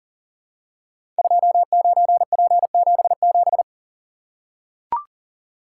Morse code, heard as a keyed tone, about 42 wpm, 700 Hz. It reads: 29P77
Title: Code Groups - 5 Letter:Numbers - 40wpm